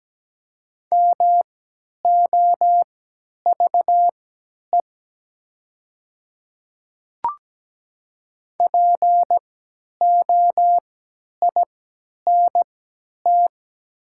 Morse code, heard as a keyed tone, about 17 wpm, spaced as for 10 wpm, 700 Hz. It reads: MOVE POINT
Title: 3x Character Spacing - Top 500 Words - 17wpm